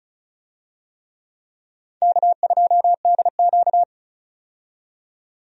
Morse code, heard as a keyed tone, about 34 wpm, 700 Hz. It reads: K2DQ
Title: USA Call Signs - 4 Characters - 35wpm